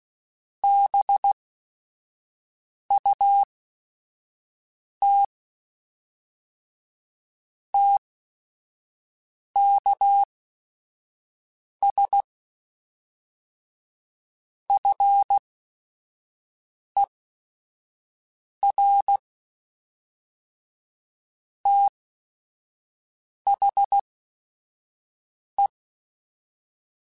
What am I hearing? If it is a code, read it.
Morse code, heard as a keyed tone, about 16 wpm, spaced as for 5 wpm, 800 Hz. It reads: BUT TKS FER THE
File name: QSOs / QSOs-mp3s / kc7sac@5